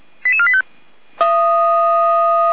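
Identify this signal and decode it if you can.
Five-tone selective call (ZVEI), heard as quick runs of five tones, tones of about 80 ms. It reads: 88485
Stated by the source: ZVEI five-tone selective call received on SDR.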